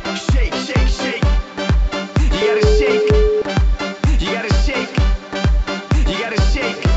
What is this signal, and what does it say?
Ringback tone (Europe, 425 Hz), with noise.